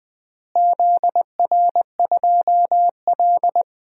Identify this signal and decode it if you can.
Morse code, heard as a keyed tone, about 20 wpm, 700 Hz. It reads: ZR2L
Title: DXCC-100 Call Signs - 4 Characters - Repeated 3x - 20wpm